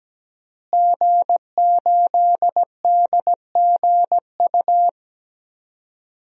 Morse code, heard as a keyed tone, about 17 wpm, 700 Hz. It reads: G8DGU
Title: DXCC-100 Call Signs - 5 Characters - Repeated 3x - 17wpm